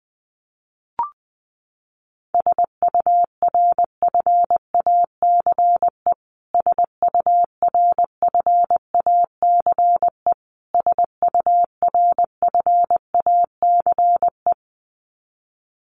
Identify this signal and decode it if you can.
Morse code, heard as a keyed tone, about 20 wpm, 700 Hz. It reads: SURFACE SURFACE SURFACE
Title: Top 500 Words Repeated 20wpm